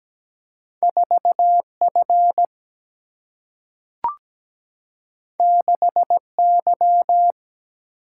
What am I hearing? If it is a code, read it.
Morse code, heard as a keyed tone, about 17 wpm, 700 Hz. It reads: 4F 6Y
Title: Single-Letter Single-Number 17wpm